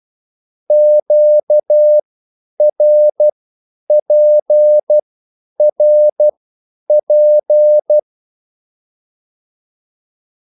Morse code, heard as a keyed tone, about 12 wpm, spaced as for 9 wpm, 600 Hz. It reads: QRPRP
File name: cwbook